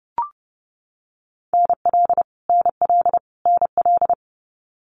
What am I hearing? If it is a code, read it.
Morse code, heard as a keyed tone, about 30 wpm, 700 Hz. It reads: NL NL NL
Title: Canadian Provinces and Territories - Repeated 3x - 30wpm